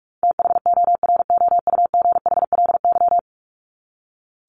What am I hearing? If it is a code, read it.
Morse code, heard as a keyed tone, about 44 wpm, 700 Hz. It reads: THOROUGHLY
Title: LY Suffix - Top 100 Words with Base Words 45wpm